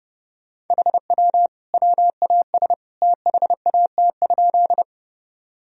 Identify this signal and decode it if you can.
Morse code, heard as a keyed tone, about 30 wpm, 700 Hz. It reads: HW WAS THAT?
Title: Sentences from Top 100 Words and Abbreviations 30wpm